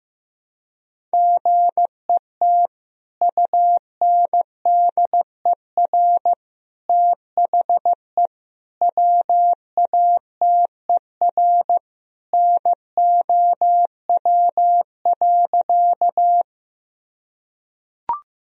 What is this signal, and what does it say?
Morse code, heard as a keyed tone, about 16 wpm, 700 Hz. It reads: GET UNDER THE WATER NOW.
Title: Sentences from Top 1000 Words - Review Part 1 - 15wpm